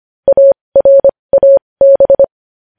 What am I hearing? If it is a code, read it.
Morse code, heard as a keyed tone, about 25 wpm, spaced as for 20 wpm, 550 Hz. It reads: ARAB
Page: News Headlines in Morse Code at 20 WPM